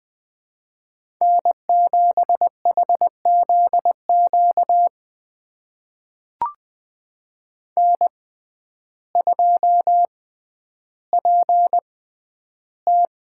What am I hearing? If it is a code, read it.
Morse code, heard as a keyed tone, about 20 wpm, spaced as for 11 wpm, 700 Hz. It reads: N7HZQ N2PT
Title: USA Call Signs - 5 Characters - 6x Character Spacing - 20wpm